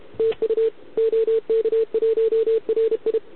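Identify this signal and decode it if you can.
Morse code, heard as a keyed tone, about 33 wpm, 440 Hz. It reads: TU OK1RI
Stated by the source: Contest CW from OK1RI on 80 m during the OKOM DX contest, recorded by a QRP station using a KX3 and a wire antenna.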